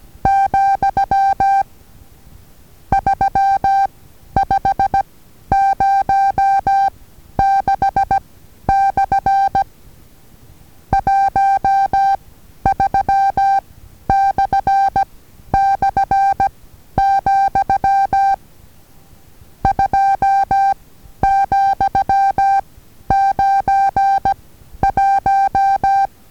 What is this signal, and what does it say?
Morse code, heard as a keyed tone, about 16 wpm, 790 Hz. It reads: , 3 5 0 6 / 1 3 / / , 2 , 9 1
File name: Taviro HA6VA A